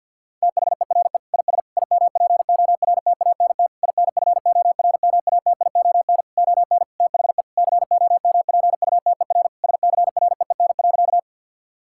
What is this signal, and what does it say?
Morse code, heard as a keyed tone, about 50 wpm, 700 Hz. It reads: THERE IS IMPORTANT INFORMATION ON THE COMPUTER SCREEN.
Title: Sentences from Top 1000 Words - Review Part 6 - 50wpm